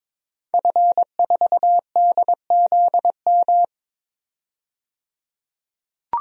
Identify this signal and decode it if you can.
Morse code, heard as a keyed tone, about 22 wpm, 700 Hz. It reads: F4DZM
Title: DXCC-100 Call Signs - 5 Characters - 22wpm